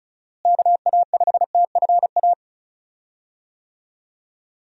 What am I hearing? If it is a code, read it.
Morse code, heard as a keyed tone, about 36 wpm, 700 Hz. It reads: KA5TFA
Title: USA Call Signs - 6 Characters - Repeated 2x - 35wpm